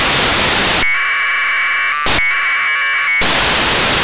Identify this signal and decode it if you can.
Packet radio (AX.25, as in APRS).
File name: antelsat 0843utc